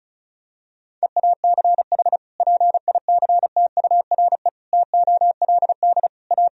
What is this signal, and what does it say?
Morse code, heard as a keyed tone, about 36 wpm, 700 Hz. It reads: EACH PICTURE TOLD A
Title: Sentences from Top 1000 Words - Review Part 1 - 35wpm